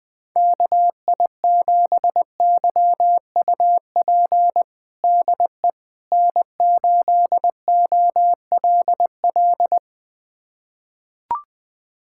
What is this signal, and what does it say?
Morse code, heard as a keyed tone, about 20 wpm, 700 Hz. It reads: KI7YUP DE N8OLL
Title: USA Callsign DE Callsign 20wpm